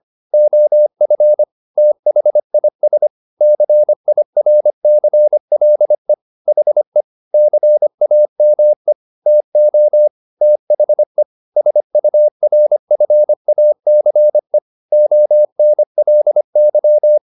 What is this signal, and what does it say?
Morse code, heard as a keyed tone, about 25 wpm, 600 Hz. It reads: OF THIS CIRCLE HE CAME TO THE SURFACE ONLY